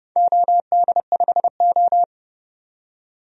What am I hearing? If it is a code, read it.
Morse code, heard as a keyed tone, about 32 wpm, 700 Hz. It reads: OD5O
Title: DXCC-100 Call Signs - 4 Characters - 30wpm